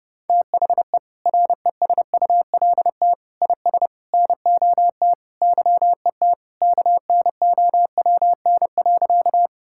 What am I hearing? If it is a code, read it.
Morse code, heard as a keyed tone, about 30 wpm, 700 Hz. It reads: THE RESULT IS NOT YET KNOWN.
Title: Sentences from Top 1000 Words - Review Part 5 - 30wpm